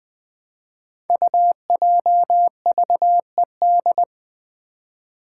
Morse code, heard as a keyed tone, about 20 wpm, 700 Hz. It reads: UJVED
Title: Five Challenging Letter-Number Combinations 20wpm